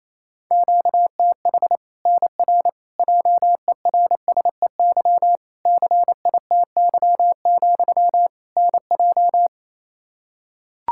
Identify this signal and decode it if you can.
Morse code, heard as a keyed tone, about 27 wpm, 700 Hz. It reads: QTH NR JERSEY CITY, NJ
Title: K1USN SST QSO Practice 28wpm